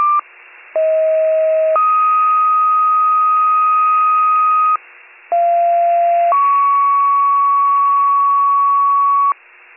Two-tone page, heard as a one-second tone, then a three-second tone, about 640 then 1250 Hz, about 690 then 1120 Hz. It is heard from 0.7 to 9.3 s.